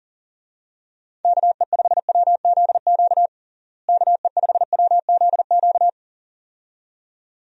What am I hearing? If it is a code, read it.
Morse code, heard as a keyed tone, about 37 wpm, 700 Hz. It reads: KE5WZQ KE5WZQ
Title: USA Call Signs - 6 Characters - Repeated 2x - 40wpm